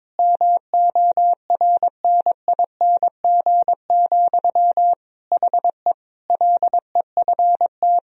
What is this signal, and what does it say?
Morse code, heard as a keyed tone, about 22 wpm, 700 Hz. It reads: MORNING, HE LEFT
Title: Sentences from Top 1000 Words - Review Part 4 - 22wpm